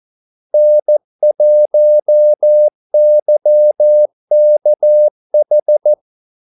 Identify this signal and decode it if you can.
Morse code, heard as a keyed tone, about 14 wpm, 600 Hz. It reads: N1YKH
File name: Callsigns-14wpm